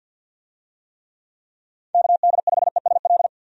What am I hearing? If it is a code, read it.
Morse code, heard as a keyed tone, about 52 wpm, 700 Hz. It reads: KD5ESL
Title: USA Call Signs - 6 Characters - 50wpm